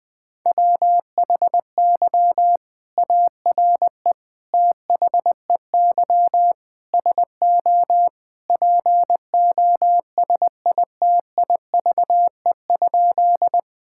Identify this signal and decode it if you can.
Morse code, heard as a keyed tone, about 20 wpm, 700 Hz. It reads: WHY ARE THEY SO POSITIVE?
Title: Sentences from Top 1000 Words - Review Part 5 - 20wpm